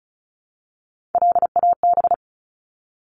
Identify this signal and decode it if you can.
Morse code, heard as a keyed tone, about 34 wpm, 700 Hz. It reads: LAB
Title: Top 1751-2000 Words 35wpm